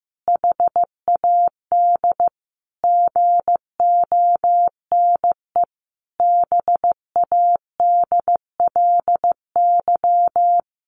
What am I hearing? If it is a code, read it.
Morse code, heard as a keyed tone, about 15 wpm, 700 Hz. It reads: HAD GONE BADLY
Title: A Sound Of Thunder 15wpm